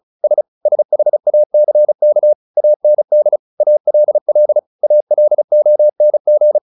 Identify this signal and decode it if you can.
Morse code, heard as a keyed tone, about 34 wpm, 600 Hz. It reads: S SHACK AND ALL ALONG
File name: the old man and the sea / the old man and the sea 1 35wpm